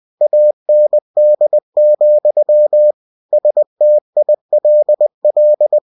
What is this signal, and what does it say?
Morse code, heard as a keyed tone, about 20 wpm, 600 Hz. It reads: AND, STILL
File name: cw-books / winnie-the-pooh / winnie-20wpm-page-0099